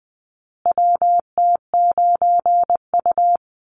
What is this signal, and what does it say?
Morse code, heard as a keyed tone, about 20 wpm, 700 Hz. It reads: WT9U
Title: Call Signs Repeated 3x - 20wpm